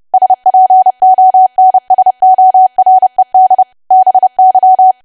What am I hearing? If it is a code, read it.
Morse code, heard as a keyed tone, about 30 wpm, 750 Hz. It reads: SPONSORED BY
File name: cwa / W1AW-Level3 / 30WPM